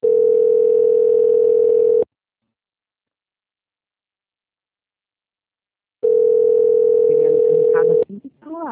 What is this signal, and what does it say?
Ringback tone (North America), 440 + 480 Hz, with speech.
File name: out- -2012- - - .16